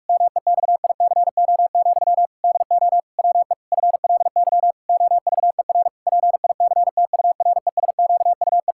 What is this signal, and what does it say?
Morse code, heard as a keyed tone, about 42 wpm, 700 Hz. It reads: MEXICO, DO WE FLY OVER PICTURESQUE
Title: Pangram Challenge v2 - 45wpm